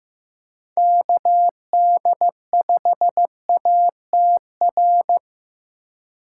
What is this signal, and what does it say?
Morse code, heard as a keyed tone, about 18 wpm, 700 Hz. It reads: KD5ATR
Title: Speed Racing - USA 2x3 Call Signs 15wpm